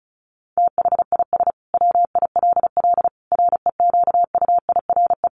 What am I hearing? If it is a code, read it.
Morse code, heard as a keyed tone, about 34 wpm, 700 Hz. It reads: THIS WILL REQUIRE
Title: Sentences from Top 900 Words 35wpm